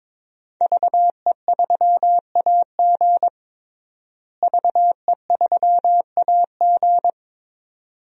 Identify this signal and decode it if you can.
Morse code, heard as a keyed tone, about 22 wpm, 700 Hz. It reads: VE3AG VE3AG
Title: DXCC-100 Call Signs - 5 Characters - Repeated 2x - 22wpm